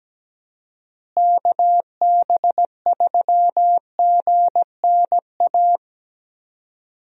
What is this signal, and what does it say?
Morse code, heard as a keyed tone, about 17 wpm, 700 Hz. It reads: KB3GNA
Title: USA Call Signs - 6 Characters - Repeated 2x - 17wpm